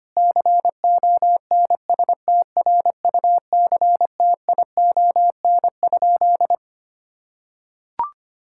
Morse code, heard as a keyed tone, about 25 wpm, 700 Hz. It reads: CONSTRUCTION?
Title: Sentences from Top 1600 Words 25wpm